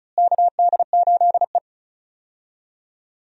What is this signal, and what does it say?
Morse code, heard as a keyed tone, about 34 wpm, 700 Hz. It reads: KD8E
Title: USA Call Signs - 4 Characters - Repeated 2x - 35wpm